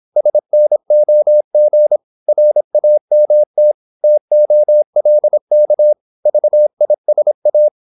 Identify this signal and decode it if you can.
Morse code, heard as a keyed tone, about 26 wpm, 600 Hz. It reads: SNOG RAMT TOLK VISA
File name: Korte ord DK 26wpm